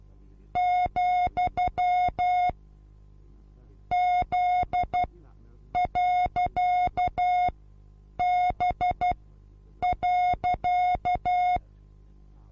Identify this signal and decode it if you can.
Morse code, heard as a keyed tone, about 12 wpm, spaced as for 8 wpm, 720 Hz. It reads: , Z.B.
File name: Onlinelehrgang / morsen / DJ4UF Morsen23